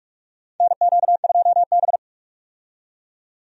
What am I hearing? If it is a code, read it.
Morse code, heard as a keyed tone, about 46 wpm, 700 Hz. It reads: NQ2B
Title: USA Call Signs - 4 Characters - 45wpm